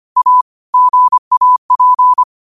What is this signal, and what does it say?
Morse code, heard as a keyed tone, about 25 wpm, 1000 Hz. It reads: A GAP